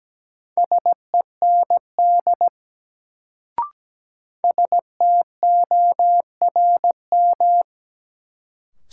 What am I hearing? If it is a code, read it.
Morse code, heard as a keyed tone, about 17 wpm, 700 Hz. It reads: SEND STORM
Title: Sentences from Top 1500 Words - New Word Drill Alt - 17wpm